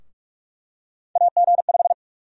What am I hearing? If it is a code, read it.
Morse code, heard as a keyed tone, about 46 wpm, 700 Hz. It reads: AG5
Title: USA Call Sign Prefixes - Repeated 2x - 45wpm